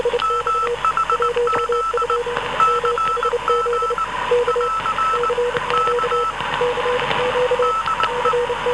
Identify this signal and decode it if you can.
Morse code, heard as a keyed tone, about 19 wpm, 1320 Hz, with noise. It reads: MHZ CG <AR> K DE